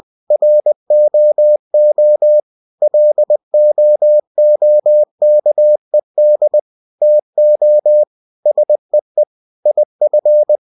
Morse code, heard as a keyed tone, about 20 wpm, 600 Hz. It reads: ROO LOOKED TO SEE IF
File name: cw-books / winnie-the-pooh / winnie-20wpm-page-0106